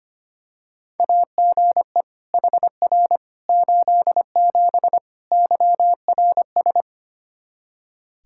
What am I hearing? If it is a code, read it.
Morse code, heard as a keyed tone, about 25 wpm, 700 Hz. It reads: AGE HR 87 YRS
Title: Practice QSO - Exchange 3 - More Info - 25wpm